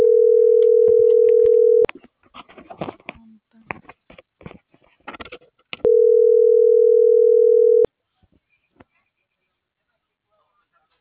Ringback tone (North America), 440 + 480 Hz.